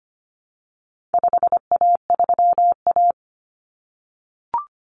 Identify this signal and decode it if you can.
Morse code, heard as a keyed tone, about 25 wpm, 700 Hz. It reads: 5A3A
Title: Common Cut Numbers - 4-Digit 25wpm